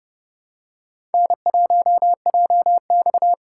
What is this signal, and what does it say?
Morse code, heard as a keyed tone, about 31 wpm, 700 Hz. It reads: N1JX
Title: USA Call Signs - 4 Characters - 4x Character Spacing - 30wpm